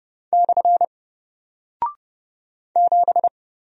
Morse code, heard as a keyed tone, about 30 wpm, 700 Hz. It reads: / 7
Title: Single Character (Letter:Number:Punctuation:Prosign) Alt 30wpm